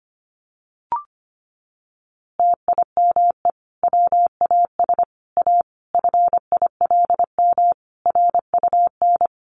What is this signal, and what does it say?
Morse code, heard as a keyed tone, about 25 wpm, 700 Hz. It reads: TIME WAS A FILM RUN
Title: A Sound Of Thunder 25wpm